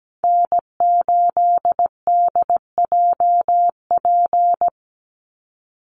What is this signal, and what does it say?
Morse code, heard as a keyed tone, about 17 wpm, 700 Hz. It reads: N8DJP
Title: USA Call Signs 17wpm